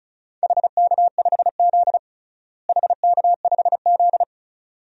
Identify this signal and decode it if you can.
Morse code, heard as a keyed tone, about 36 wpm, 700 Hz. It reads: HK5Z HK5Z
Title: DXCC-100 Call Signs - 4 Characters - Repeated 2x - 35wpm